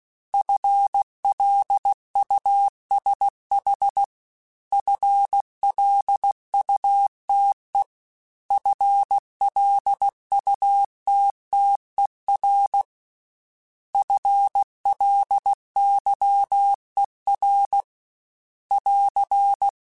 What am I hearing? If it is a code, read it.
Morse code, heard as a keyed tone, about 16 wpm, 800 Hz. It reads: FLUSH FLUTE FLUTTER FLYER <AR>
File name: FL Runs / FL Runs mp3s / FL Runs @15